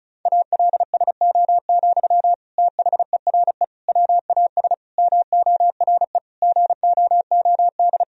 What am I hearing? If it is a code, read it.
Morse code, heard as a keyed tone, about 34 wpm, 700 Hz. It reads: ALSO, THERE WAS MORE GOOD
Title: Sentences from Top 1000 Words - Review Part 1 - 35wpm